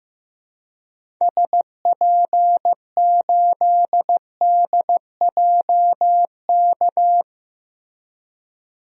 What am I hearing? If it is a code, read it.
Morse code, heard as a keyed tone, about 16 wpm, 700 Hz. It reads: SP8DJK
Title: DXCC-100 Call Signs - 6 Characters - Repeated 3x - 15wpm